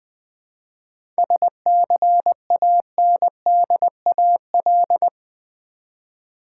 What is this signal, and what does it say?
Morse code, heard as a keyed tone, about 20 wpm, 700 Hz. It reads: SCANDAL
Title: 7 Letter Words - First 3 Letters Given - 20wpm